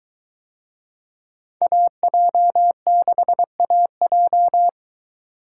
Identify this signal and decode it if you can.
Morse code, heard as a keyed tone, about 21 wpm, 700 Hz. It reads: AJ6AJ